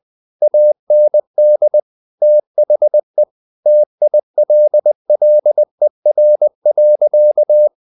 Morse code, heard as a keyed tone, about 20 wpm, 600 Hz. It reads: AND THE TILLER.
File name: the old man and the sea part 2 20wpm